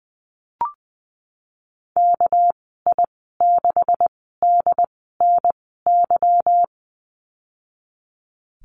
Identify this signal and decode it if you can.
Morse code, heard as a keyed tone, about 20 wpm, 700 Hz. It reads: K I 6 D N Y
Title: 231 Lesson 37g - Letter x - Call Signs 20wpm